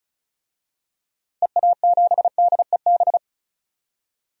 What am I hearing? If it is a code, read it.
Morse code, heard as a keyed tone, about 36 wpm, 700 Hz. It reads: EA7DEB
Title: DXCC-100 Call Signs - 6 Characters - Repeated 2x - 35wpm